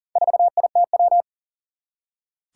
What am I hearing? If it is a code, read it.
Morse code, heard as a keyed tone, about 40 wpm, 700 Hz. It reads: 4ITW
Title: Four Challenging Letter-Number Combinations 40wpm